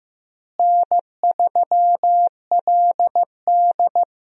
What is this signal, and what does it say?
Morse code, heard as a keyed tone, about 15 wpm, 700 Hz. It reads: N3LD
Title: USA Call Signs - 4 Characters - Repeated 2x - 15wpm